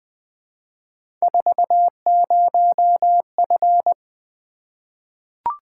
Morse code, heard as a keyed tone, about 20 wpm, 700 Hz. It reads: 40F
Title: Three Letter-Number Combinations - 20wpm